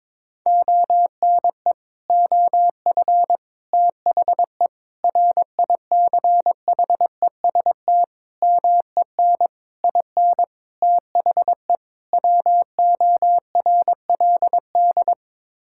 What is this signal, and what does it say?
Morse code, heard as a keyed tone, about 22 wpm, 700 Hz. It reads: ONE OF THE RICHEST MEN IN THE WORLD
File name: practice / 22wpm / Sets of 8 Words - 22wpm